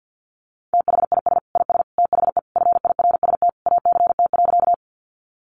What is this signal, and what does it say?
Morse code, heard as a keyed tone, about 50 wpm, 700 Hz. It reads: THIS IS THE FIRST ACT.
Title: Sentences from Top 200 Words 50wpm